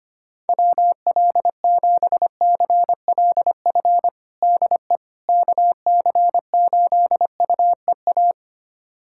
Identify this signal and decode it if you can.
Morse code, heard as a keyed tone, about 25 wpm, 700 Hz. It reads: WL7CLF DE KC8UEA
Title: USA Callsign DE Callsign - 6 Characters - 25wpm